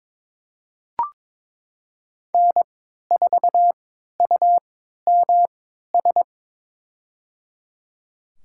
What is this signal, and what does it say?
Morse code, heard as a keyed tone, about 22 wpm, spaced as for 12 wpm, 700 Hz. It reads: N4UMS
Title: USA Call Signs - 5 Characters - 3x Character Spacing - 22wpm